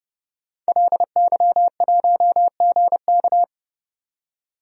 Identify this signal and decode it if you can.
Morse code, heard as a keyed tone, about 30 wpm, 700 Hz. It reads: LY1GK
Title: DXCC-100 Call Signs - 5 Characters - 30wpm